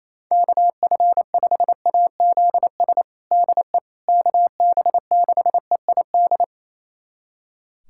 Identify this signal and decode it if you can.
Morse code, heard as a keyed tone, about 28 wpm, 700 Hz. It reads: KF5AZS DE KB6EID
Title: USA Callsign DE Callsign - 6 Characters - 28wpm